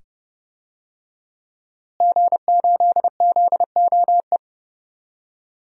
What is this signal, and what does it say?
Morse code, heard as a keyed tone, about 30 wpm, 700 Hz. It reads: G8ZOE
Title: DXCC-100 Call Signs - 5 Characters - Repeated 2x - 30wpm